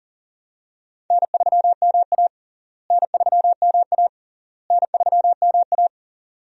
Morse code, heard as a keyed tone, about 40 wpm, 700 Hz. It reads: N3MA N3MA N3MA
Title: USA Call Signs - 4 Characters - Repeated 3x - 40wpm